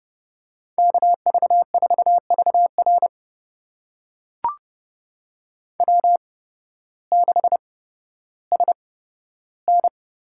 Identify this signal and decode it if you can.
Morse code, heard as a keyed tone, about 30 wpm, spaced as for 14 wpm, 700 Hz. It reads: KV4VR W6SN
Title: USA Call Signs - 8x Character Spacing - 30wpm